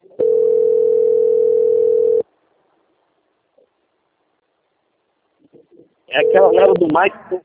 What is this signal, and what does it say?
Ringback tone (North America), 440 + 480 Hz, with speech.
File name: out- - - - - .9374